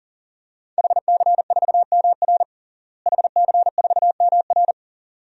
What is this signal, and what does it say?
Morse code, heard as a keyed tone, about 40 wpm, 690 Hz. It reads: HC4MR HC4MR
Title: DXCC-100 Call Signs - 5 Characters - Repeated 2x - 40wpm